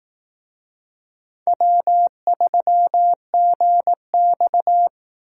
Morse code, heard as a keyed tone, about 18 wpm, 700 Hz. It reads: W3GX
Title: Speed Racing - USA 1x2 Call Signs 15wpm